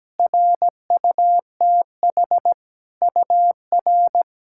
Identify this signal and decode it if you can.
Morse code, heard as a keyed tone, about 17 wpm, 700 Hz. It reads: RUTH UR
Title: SOTA QSO Practice 17wpm